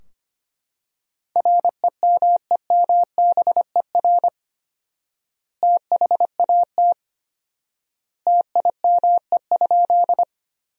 Morse code, heard as a keyed tone, about 25 wpm, 700 Hz. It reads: REMEMBER THAT TIME?
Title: 4x Word-Spacing - Sentences from Top 400 Words - 25wpm